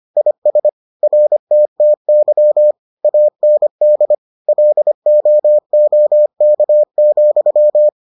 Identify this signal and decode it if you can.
Morse code, heard as a keyed tone, about 25 wpm, 600 Hz. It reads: IS RTTY AND LOOK,